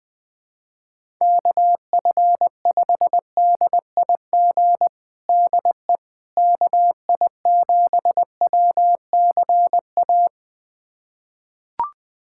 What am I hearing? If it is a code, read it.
Morse code, heard as a keyed tone, about 20 wpm, 700 Hz. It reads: KF5DIG DE KI7WCA
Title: USA Callsign DE Callsign 20wpm